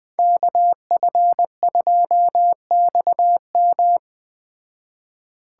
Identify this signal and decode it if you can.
Morse code, heard as a keyed tone, about 20 wpm, 700 Hz. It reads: KF2XM
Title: Speed Racing - USA 2x2 Call Signs 17wpm